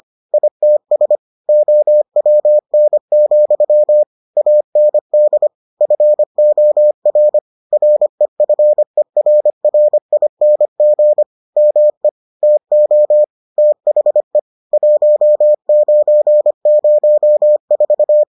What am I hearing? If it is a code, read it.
Morse code, heard as a keyed tone, about 25 wpm, 600 Hz. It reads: ITS OWN, AND FOR REFERRING ME TO THE 1904